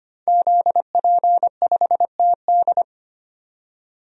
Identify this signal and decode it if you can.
Morse code, heard as a keyed tone, about 25 wpm, 700 Hz. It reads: ZP5TD
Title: DXCC-100 Call Signs - 5 Characters - 25wpm